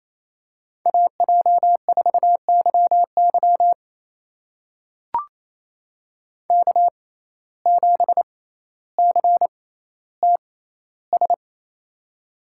Morse code, heard as a keyed tone, about 28 wpm, spaced as for 14 wpm, 700 Hz. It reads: AJ4YY K7CTS
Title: USA Call Signs - 6x Character Spacing - 28wpm